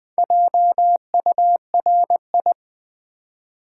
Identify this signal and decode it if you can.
Morse code, heard as a keyed tone, about 20 wpm, 700 Hz. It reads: JURI
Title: Build A Word - 4 Letters At A Time - 12-Letter Words - 20wpm